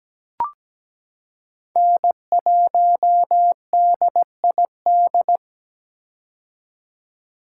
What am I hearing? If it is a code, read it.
Morse code, heard as a keyed tone, about 17 wpm, 700 Hz. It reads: N1DID
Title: K1USN SST QSO Practice - 17wpm